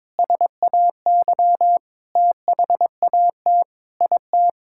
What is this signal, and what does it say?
Morse code, heard as a keyed tone, about 22 wpm, 700 Hz. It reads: SAY THAT IT
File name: Sentences with 9-Word Phrases - 22wpm